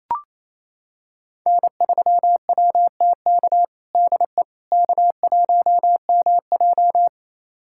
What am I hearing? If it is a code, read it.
Morse code, heard as a keyed tone, about 28 wpm, 700 Hz. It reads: N3WTK DE K1MJ
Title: USA Callsign DE Callsign 28wpm